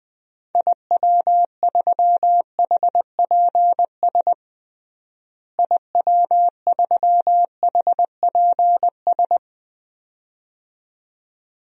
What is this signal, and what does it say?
Morse code, heard as a keyed tone, about 20 wpm, 700 Hz. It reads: IW3HPS IW3HPS
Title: DXCC-100 Call Signs - 6 Characters - Repeated 2x - 20wpm